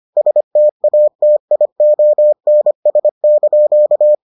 Morse code, heard as a keyed tone, about 25 wpm, 600 Hz. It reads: STATIONS)